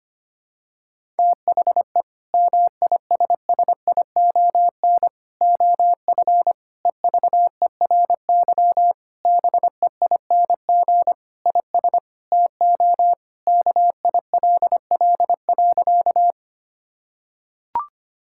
Morse code, heard as a keyed tone, about 25 wpm, 700 Hz. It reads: THE MISSION OF EVERY BEING IS TO KILL.
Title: Sentences from Top 1600 Words 25wpm